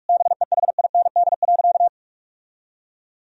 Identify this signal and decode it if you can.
Morse code, heard as a keyed tone, about 46 wpm, 700 Hz. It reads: BEHIND.
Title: Sentences from Top 1000 Words - Review Part 1 - 45wpm